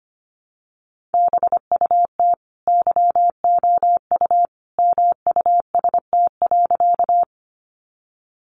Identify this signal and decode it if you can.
Morse code, heard as a keyed tone, about 25 wpm, 700 Hz. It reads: BUT YOU MUST.
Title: Sentences from Top 200 Words 25wpm